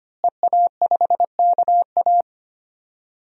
Morse code, heard as a keyed tone, about 25 wpm, 700 Hz. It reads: EA5KA
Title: DXCC-100 Call Signs - 5 Characters - Repeated 3x - 25wpm